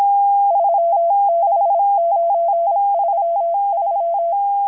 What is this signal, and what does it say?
Morse code, heard as a keyed tone, about 26 wpm, 680 Hz. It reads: 36933
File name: HM02- -0437utc-7351khz